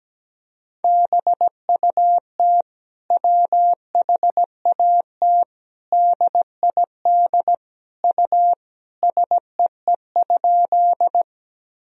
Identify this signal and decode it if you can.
Morse code, heard as a keyed tone, about 17 wpm, 700 Hz. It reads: BUT WHAT DID U SEE?
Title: Sentences from Top 100 Words and Abbreviations 17wpm